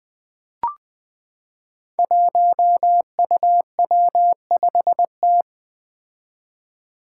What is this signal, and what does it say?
Morse code, heard as a keyed tone, about 20 wpm, 700 Hz. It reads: 1UW5T
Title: Five Challenging Letter-Number Combinations 20wpm